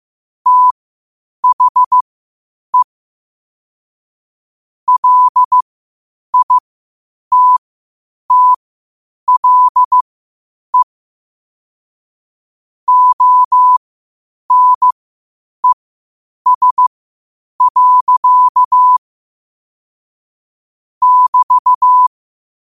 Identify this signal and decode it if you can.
Morse code, heard as a keyed tone, about 15 wpm, spaced as for 8 wpm, 1000 Hz. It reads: THE LITTLE ONES. <BT>
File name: qotdfiles / 8wpm